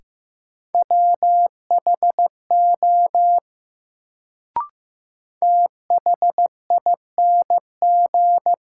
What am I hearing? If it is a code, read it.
Morse code, heard as a keyed tone, about 15 wpm, 700 Hz. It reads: WHO THING
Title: Sentences from Top 100 Words - New Word Drill - 15wpm